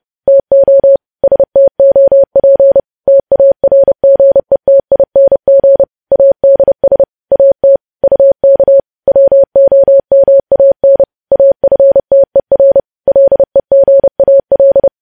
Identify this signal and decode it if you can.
Morse code, heard as a keyed tone, about 30 wpm, 550 Hz. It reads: TO STOP TARGETING ADS AT UK WOMAN AFTER LEGAL
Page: news headlines in morse code at 30 wpm